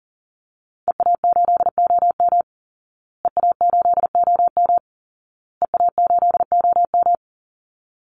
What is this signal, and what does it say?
Morse code, heard as a keyed tone, about 40 wpm, 700 Hz. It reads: EA8OM EA8OM EA8OM
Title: Call Signs Repeated 2x - 40wpm